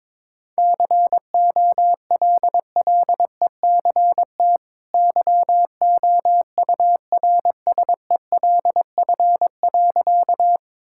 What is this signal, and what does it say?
Morse code, heard as a keyed tone, about 21 wpm, 700 Hz. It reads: COLLECT YOURSELF.
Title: Sentences from Top 1600 Words 22wpm